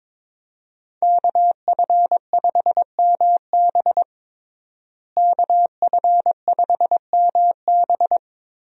Morse code, heard as a keyed tone, about 22 wpm, 700 Hz. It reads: KF5MB KF5MB
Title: USA Call Signs - 5 Characters - Repeated 2x - 22wpm